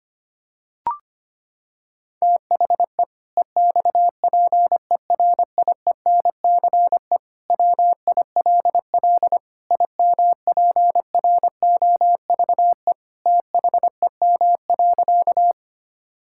Morse code, heard as a keyed tone, about 25 wpm, 700 Hz. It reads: THE EXPERIENCE WILL IMPROVE THEM.
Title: Sentences from Top 1000 Words - Review Part 5 - 25wpm